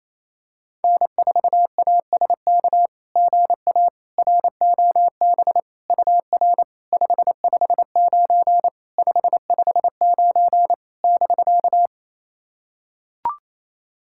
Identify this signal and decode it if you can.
Morse code, heard as a keyed tone, about 28 wpm, 700 Hz. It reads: N4ASK GA ROB UR 559 559 <BK>
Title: SOTA QSO Practice 28wpm